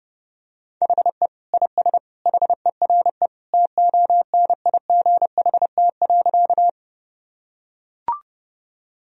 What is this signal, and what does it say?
Morse code, heard as a keyed tone, about 31 wpm, 700 Hz. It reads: HE IS HERE TONIGHT.
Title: Sentences from Top 1000 Words - Review Part 3 - 30wpm